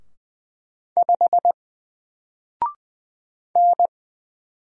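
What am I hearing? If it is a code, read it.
Morse code, heard as a keyed tone, about 20 wpm, 700 Hz. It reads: 5 N
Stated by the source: Single letter and number Morse drill at 20 wpm, from N4LCD.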